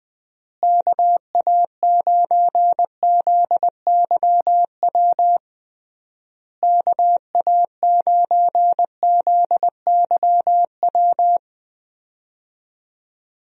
Morse code, heard as a keyed tone, about 20 wpm, 700 Hz. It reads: KA9ZYW KA9ZYW
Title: USA Call Signs - 6 Characters - Repeated 3x - 20wpm